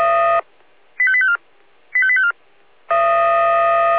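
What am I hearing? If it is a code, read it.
Five-tone selective call (ZVEI), heard as quick runs of five tones, tones of about 70 ms. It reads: 85854 85854
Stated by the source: ZVEI five-tone selective call 85880, received by SDR.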